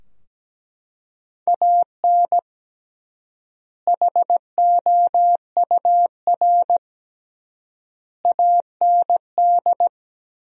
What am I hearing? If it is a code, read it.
Morse code, heard as a keyed tone, about 17 wpm, 700 Hz. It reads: AN HOUR AND
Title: 3x Word-Spacing - Sentences from Top 400 Words - 17wpm